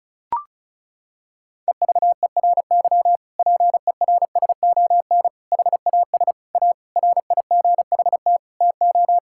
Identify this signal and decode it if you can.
Morse code, heard as a keyed tone, about 36 wpm, 700 Hz. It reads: EVERY PERSON HAS A RIGHT TO
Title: Sentences from Top 1000 Words - Review Part 5 - 35wpm